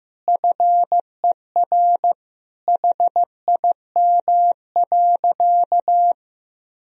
Morse code, heard as a keyed tone, about 15 wpm, 700 Hz. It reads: FER HIM.
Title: Sentences from Top 500 Words and Abbreviations 15wpm